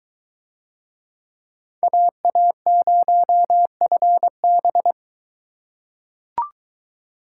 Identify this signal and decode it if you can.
Morse code, heard as a keyed tone, about 22 wpm, 700 Hz. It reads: AA0FB